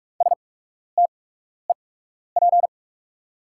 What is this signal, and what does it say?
Morse code, heard as a keyed tone, about 44 wpm, spaced as for 12 wpm, 700 Hz. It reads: STEP
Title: 8x Character Spacing - Top 500 Words - 45wpm